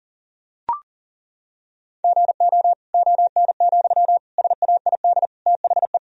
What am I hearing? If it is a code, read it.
Morse code, heard as a keyed tone, about 40 wpm, 700 Hz. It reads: GO ON, SAID THE
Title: Sentences from Top 1000 Words - Review Part 4 - 40wpm